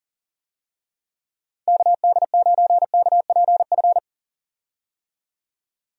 Morse code, heard as a keyed tone, about 40 wpm, 700 Hz. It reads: KD9KPF
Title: USA Call Signs - 6 Characters - 40wpm